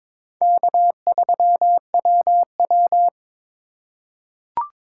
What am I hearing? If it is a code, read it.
Morse code, heard as a keyed tone, about 22 wpm, 700 Hz. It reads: K3WW
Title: USA Call Signs - 4 Characters - 2x Character Spacing - 22wpm